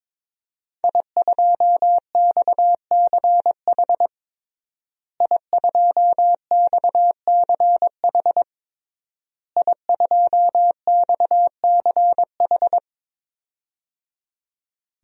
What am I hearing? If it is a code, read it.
Morse code, heard as a keyed tone, about 22 wpm, 700 Hz. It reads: I2XCH I2XCH I2XCH
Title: DXCC-100 Call Signs - 5 Characters - Repeated 3x - 22wpm